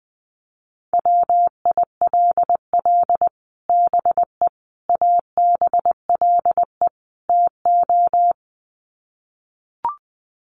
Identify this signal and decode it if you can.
Morse code, heard as a keyed tone, about 20 wpm, 700 Hz. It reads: WILL BE ABLE TO
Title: Sets of 4 Words 20wpm